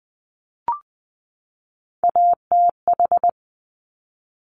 Morse code, heard as a keyed tone, about 20 wpm, 700 Hz. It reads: ATH
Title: Top Three Letter Combinations in Words 20wpm